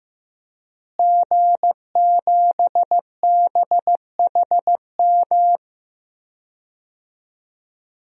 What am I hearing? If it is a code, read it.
Morse code, heard as a keyed tone, about 15 wpm, 700 Hz. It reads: G7BHM
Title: DXCC-100 Call Signs - 5 Characters - Repeated 3x - 15wpm